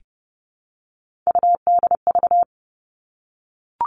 Morse code, heard as a keyed tone, about 28 wpm, 700 Hz. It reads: UDV
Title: All Cut Numbers - 3-Digit 30wpm